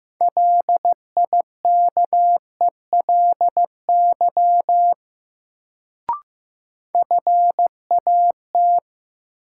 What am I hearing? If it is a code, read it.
Morse code, heard as a keyed tone, about 15 wpm, 700 Hz. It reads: LIKELY FAT
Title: Sentences from Top 600 Words - New Word Drill - 15wpm